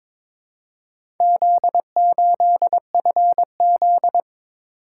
Morse code, heard as a keyed tone, about 23 wpm, 700 Hz. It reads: Z8FZ
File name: practice / 22wpm / Four Letter-Number Combinations - 22wpm